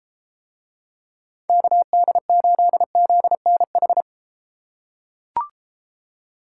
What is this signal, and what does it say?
Morse code, heard as a keyed tone, about 32 wpm, 700 Hz. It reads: KD8ZNH